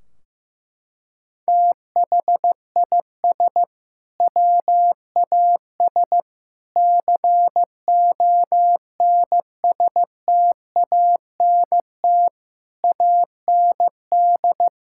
Morse code, heard as a keyed tone, about 16 wpm, 700 Hz. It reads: THIS WAS CONSTANT AND
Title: Sentences from Top 1600 Words 15wpm